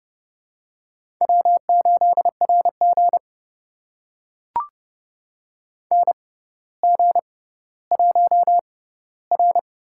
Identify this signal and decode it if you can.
Morse code, heard as a keyed tone, about 30 wpm, spaced as for 15 wpm, 700 Hz. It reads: W8RG NG1R
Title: USA Call Signs - 4 Characters - 6x Character Spacing - 30wpm